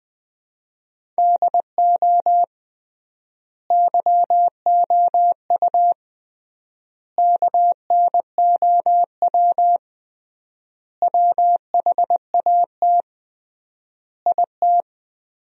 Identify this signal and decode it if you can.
Morse code, heard as a keyed tone, about 20 wpm, 700 Hz. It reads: DO YOU KNOW WHAT IT
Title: 3x Word-Spacing - Sentences from Top 100 Words - 20wpm